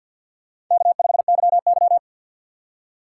Morse code, heard as a keyed tone, about 47 wpm, 700 Hz. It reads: K5YY
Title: USA Call Signs - 2x Character Spacing - 50wpm